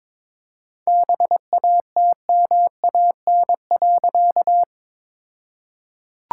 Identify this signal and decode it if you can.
Morse code, heard as a keyed tone, about 22 wpm, 700 Hz. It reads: BATMAN.
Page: morse-code-ninja-practice-22wpm